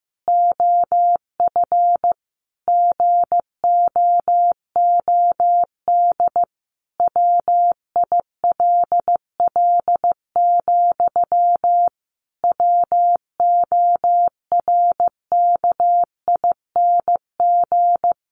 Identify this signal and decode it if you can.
Morse code, heard as a keyed tone, about 15 wpm, 700 Hz. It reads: OF GOOD WILL, WORKING
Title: Sentences from Top 1400 Words 15wpm